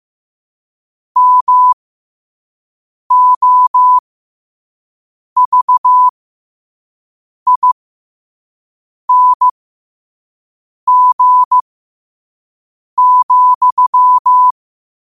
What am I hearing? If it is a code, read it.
Morse code, heard as a keyed tone, about 15 wpm, spaced as for 5 wpm, 1000 Hz. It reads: MOVING,
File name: qotdfiles / 5wpm